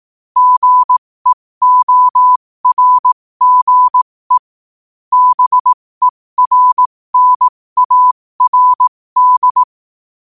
Morse code, heard as a keyed tone, about 18 wpm, spaced as for 15 wpm, 1000 Hz. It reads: GEORGE BERNARD
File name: 15wpm